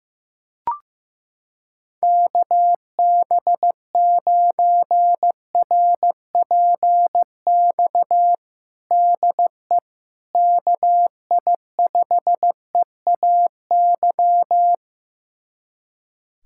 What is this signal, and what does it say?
Morse code, heard as a keyed tone, about 15 wpm, 700 Hz. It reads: KB9RPX DE KI5EAY
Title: USA Callsign DE Callsign - 6 Characters - 15wpm